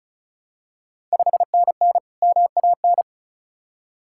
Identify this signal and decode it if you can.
Morse code, heard as a keyed tone, about 36 wpm, 700 Hz. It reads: 5NN MAN
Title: Ontario QSO Party Exchange 35wpm